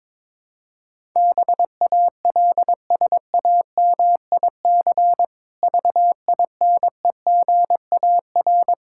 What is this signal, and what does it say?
Morse code, heard as a keyed tone, about 22 wpm, 700 Hz. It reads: BALSAMIC VINEGAR
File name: Difficult Words with Hints - 22wpm